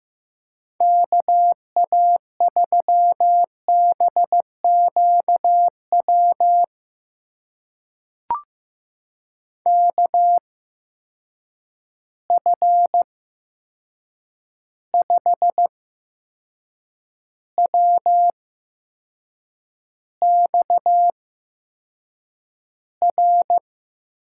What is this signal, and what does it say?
Morse code, heard as a keyed tone, about 15 wpm, spaced as for 6 wpm, 700 Hz. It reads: KA3BQW KF5WXR
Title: USA Call Signs - 6 Characters - 8x Character Spacing - 15wpm